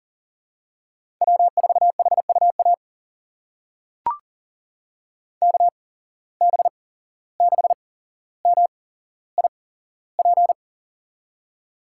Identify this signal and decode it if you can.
Morse code, heard as a keyed tone, about 38 wpm, spaced as for 16 wpm, 700 Hz. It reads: W4HUA KB6MIP
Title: USA Call Signs - 8x Character Spacing - 40wpm